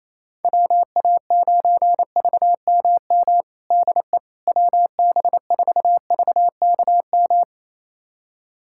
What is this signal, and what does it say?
Morse code, heard as a keyed tone, about 28 wpm, 700 Hz. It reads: WA9VMM DE WB4VKM
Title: USA Callsign DE Callsign - 6 Characters - 28wpm